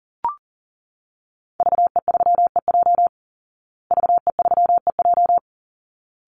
Morse code, heard as a keyed tone, about 37 wpm, 700 Hz. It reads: VE3EJ VE3EJ
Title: Call Signs Repeated 3x - 40wpm